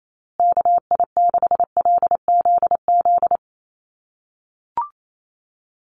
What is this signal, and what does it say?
Morse code, heard as a keyed tone, about 28 wpm, 700 Hz. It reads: KI6LZZ
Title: USA Call Signs Repeated 3x 28wpm